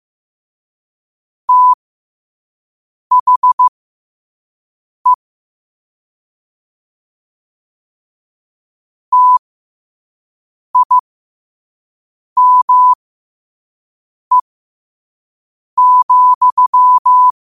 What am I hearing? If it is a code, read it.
Morse code, heard as a keyed tone, about 15 wpm, spaced as for 5 wpm, 1000 Hz. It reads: THE TIME,
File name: qotdfiles / 5wpm